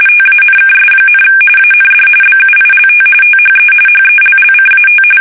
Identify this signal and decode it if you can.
RTTY, about 45.45 baud. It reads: TANKS, AND BUILT THE HELLCAT
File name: fsk850